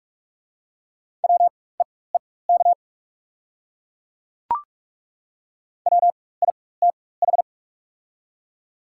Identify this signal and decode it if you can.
Morse code, heard as a keyed tone, about 42 wpm, spaced as for 21 wpm, 690 Hz. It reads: WEEK WITH
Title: 4x Character Spacing - Top 500 Words - 45wpm